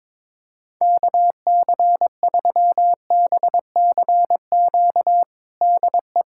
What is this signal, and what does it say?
Morse code, heard as a keyed tone, about 21 wpm, 700 Hz. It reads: KC3BCQ DE
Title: USA Callsign DE Callsign 22wpm